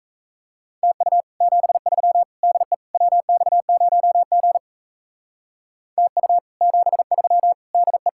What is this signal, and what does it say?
Morse code, heard as a keyed tone, about 36 wpm, 700 Hz. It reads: TU 73 DE WX0G TU 73 DE